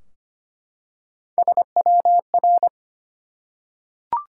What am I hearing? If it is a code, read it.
Morse code, heard as a keyed tone, about 25 wpm, 700 Hz. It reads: SWR
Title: Top 180 QSO Elements 25wpm